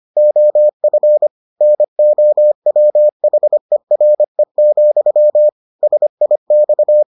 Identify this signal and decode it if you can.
Morse code, heard as a keyed tone, about 25 wpm, 600 Hz. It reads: OF NOWHERE, SIX